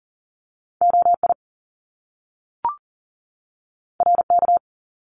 Morse code, heard as a keyed tone, about 39 wpm, 700 Hz. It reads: OI RK
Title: Top Two Letter Combinations in Words 40wpm